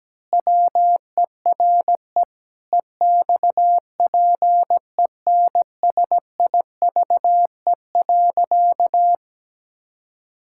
Morse code, heard as a keyed tone, about 17 wpm, 700 Hz. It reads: WERE EXPENSIVE.
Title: Pangram Challenge v2 - 17wpm